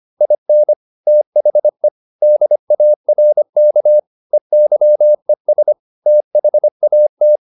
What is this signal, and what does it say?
Morse code, heard as a keyed tone, about 25 wpm, 600 Hz. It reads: IN THE DARK EYES THAT